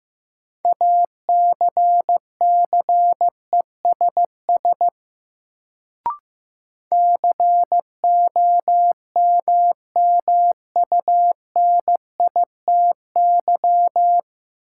Morse code, heard as a keyed tone, about 15 wpm, 700 Hz. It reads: ACCESS COMMUNITY
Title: Sentences from Top 1000 Words - Review Part 4 - New Word Drill - 15wpm